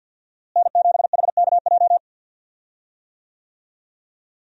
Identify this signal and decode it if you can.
Morse code, heard as a keyed tone, about 48 wpm, 700 Hz. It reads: N7HKJ
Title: USA Call Signs - 5 Characters - Repeated 3x - 50wpm